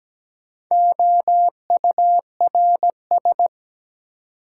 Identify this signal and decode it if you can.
Morse code, heard as a keyed tone, about 17 wpm, 700 Hz. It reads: OURS
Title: Sentences from Top 1000 Words - Review Part 5 - New Word Drill - 17wpm